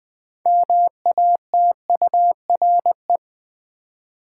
Morse code, heard as a keyed tone, about 20 wpm, 700 Hz. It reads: MATURE
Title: 6 Letter Words - First 3 Letters Given - 20wpm